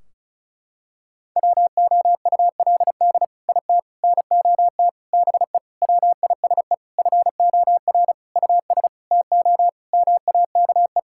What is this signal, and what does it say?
Morse code, heard as a keyed tone, about 36 wpm, 700 Hz. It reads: WOULD IT NOT BE WISE FOR US TO MAKE
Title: Sentences from Top 1600 Words 35wpm